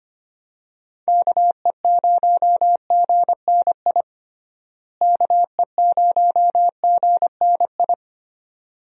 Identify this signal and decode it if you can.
Morse code, heard as a keyed tone, about 25 wpm, 700 Hz. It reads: KE0GNI KE0GNI
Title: USA Call Signs - 6 Characters - Repeated 2x - 25wpm